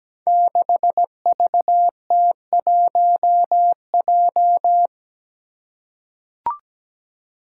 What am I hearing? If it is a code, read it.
Morse code, heard as a keyed tone, about 17 wpm, 700 Hz. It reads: 6VT1J
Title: Five Challenging Letter-Number Combinations 17wpm